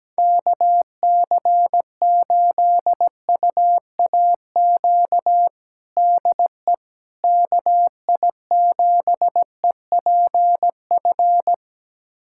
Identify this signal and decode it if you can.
Morse code, heard as a keyed tone, about 17 wpm, 700 Hz. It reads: KC8UAQ DE KI7EPF
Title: USA Callsign DE Callsign - 6 Characters - 17wpm